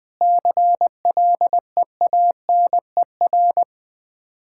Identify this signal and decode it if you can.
Morse code, heard as a keyed tone, about 20 wpm, 700 Hz. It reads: CLEANER
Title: Easy Words - Speed-Racing with Hints - 20wpm